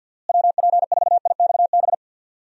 Morse code, heard as a keyed tone, about 52 wpm, 700 Hz. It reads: WP4IXB
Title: USA Call Signs - 6 Characters - Repeated 2x - 50wpm